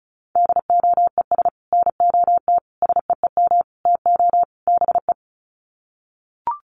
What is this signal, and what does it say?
Morse code, heard as a keyed tone, about 34 wpm, 700 Hz. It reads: DOES NOT SEEM TO BE
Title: Sets of 5 Words 35wpm